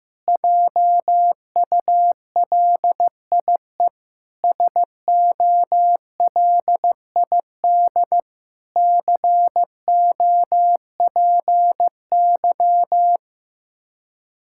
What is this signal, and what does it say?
Morse code, heard as a keyed tone, about 15 wpm, 700 Hz. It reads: JULIE SOLID COPY
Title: Practice QSO - Exchange 3 - More Info - 15wpm